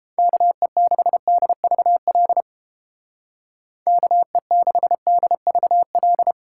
Morse code, heard as a keyed tone, about 30 wpm, 700 Hz. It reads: KE6DVL KE6DVL